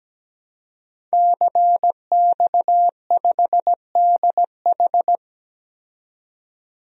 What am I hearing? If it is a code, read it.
Morse code, heard as a keyed tone, about 17 wpm, 700 Hz. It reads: CX5DH
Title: DXCC-100 Call Signs - 5 Characters - 17wpm